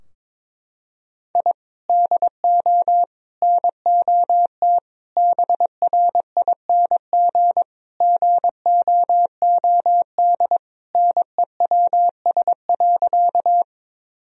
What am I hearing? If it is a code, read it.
Morse code, heard as a keyed tone, about 22 wpm, 700 Hz. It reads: I DO NOT BRING GOOD NEWS.
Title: Sentences from Top 1000 Words - Review Part 3 - 22wpm